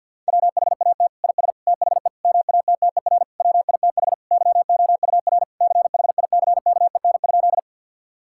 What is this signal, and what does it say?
Morse code, heard as a keyed tone, about 50 wpm, 700 Hz. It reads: WHAT IS THE MATTER WITH YOUR CHICKEN?
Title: Sentences from Top 1000 Words - Review Part 5 - 50wpm